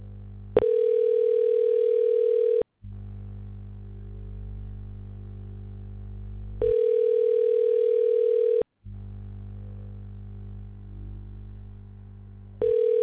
Ringback tone (North America), 440 + 480 Hz.
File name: recordings / RE80487cdccdc7f4c1fdeb2e9c1ae9a892